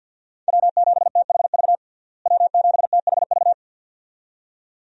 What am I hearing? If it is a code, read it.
Morse code, heard as a keyed tone, about 49 wpm, 700 Hz. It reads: W7THV W7THV
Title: USA Call Signs - 5 Characters - Repeated 2x - 50wpm